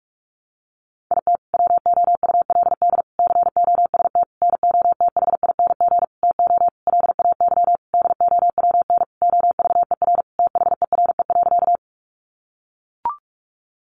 Morse code, heard as a keyed tone, about 44 wpm, 700 Hz. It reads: IT WOULD COST NOTHING TO LAY DOWN OVER THERE.
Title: Sentences from Top 800 Words 45wpm